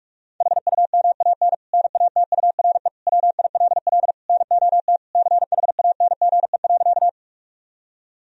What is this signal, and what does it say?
Morse code, heard as a keyed tone, about 41 wpm, 700 Hz. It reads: HUMAN NATURE WILL NOT CHANGE.
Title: Sentences from Top 1000 Words - Review Part 3 - 45wpm